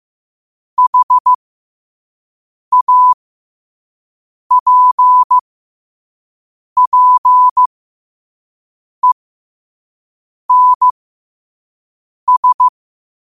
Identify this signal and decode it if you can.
Morse code, heard as a keyed tone, about 15 wpm, spaced as for 5 wpm, 1000 Hz. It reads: HAPPENS
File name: qotdfiles / 5wpm